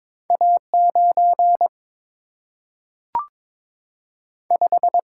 Morse code, heard as a keyed tone, about 22 wpm, 700 Hz. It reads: A 9 5
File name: practice / 22wpm / Two Letter-Number Combinations - 22wpm